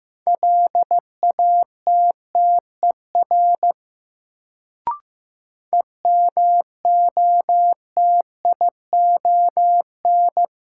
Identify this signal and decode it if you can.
Morse code, heard as a keyed tone, about 15 wpm, 700 Hz. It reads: LATTER EMOTION
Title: Sentences from Top 1600 Words - New Word Drill Alt - 15wpm